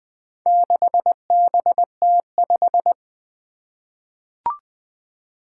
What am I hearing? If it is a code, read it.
Morse code, heard as a keyed tone, about 20 wpm, 700 Hz. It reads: 6BT5
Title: Four Challenging Letter-Number Combinations 20wpm